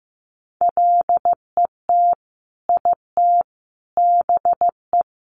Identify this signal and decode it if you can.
Morse code, heard as a keyed tone, about 15 wpm, 700 Hz. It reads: LET IT BE
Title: Sentences from Top 1100 Words 15wpm